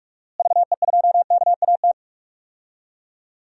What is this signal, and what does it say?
Morse code, heard as a keyed tone, about 45 wpm, 700 Hz. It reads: VE2KAT
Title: DXCC-100 Call Signs - 6 Characters - Repeated 2x - 45wpm